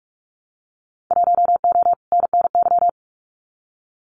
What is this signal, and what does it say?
Morse code, heard as a keyed tone, about 44 wpm, 690 Hz. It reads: 1O NNY
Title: Winter Field Day Exchanges Encore 45wpm